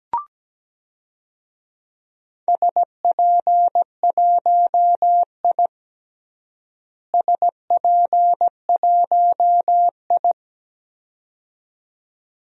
Morse code, heard as a keyed tone, about 17 wpm, 700 Hz. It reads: SP1I SP1I
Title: DXCC-100 Call Signs - 4 Characters - Repeated 2x - 17wpm